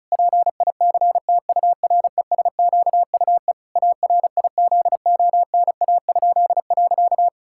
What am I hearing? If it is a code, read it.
Morse code, heard as a keyed tone, about 34 wpm, 700 Hz. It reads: PICTURESQUE ARIZONA?.
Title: Pangram Challenge v2 - 35wpm